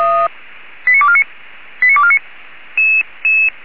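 Five-tone selective call (ZVEI), heard as quick runs of five tones, tones of about 70 ms. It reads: 89249 89249
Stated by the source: ZVEI five-tone call received by SDR.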